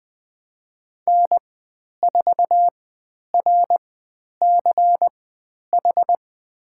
Morse code, heard as a keyed tone, about 20 wpm, spaced as for 10 wpm, 700 Hz. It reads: N4RCH
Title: 102 Lesson 17g - Letter u - Call Signs 20wpm